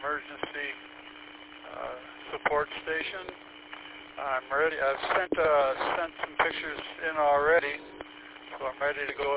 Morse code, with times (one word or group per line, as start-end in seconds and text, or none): none